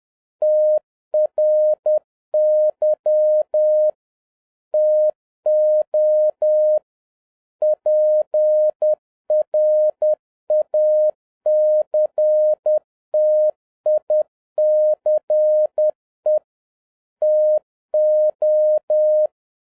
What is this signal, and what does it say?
Morse code, heard as a keyed tone, about 10 wpm, 610 Hz. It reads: TRY TO PRACTICE TO